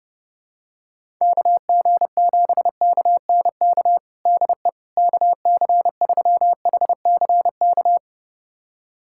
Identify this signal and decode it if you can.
Morse code, heard as a keyed tone, about 32 wpm, 700 Hz. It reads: KG7KNK DE KC3HCK
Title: USA Callsign DE Callsign - 6 Characters - 30wpm